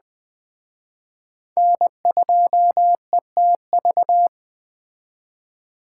Morse code, heard as a keyed tone, about 21 wpm, 700 Hz. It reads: N2ETV
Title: Speed Racing - USA 1x3 Call Signs 17wpm